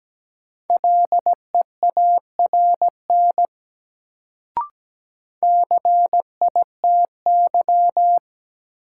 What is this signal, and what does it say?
Morse code, heard as a keyed tone, about 17 wpm, 700 Hz. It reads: LEARN CITY
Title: Sentences from Top 300 Words - New Word Drill - 17wpm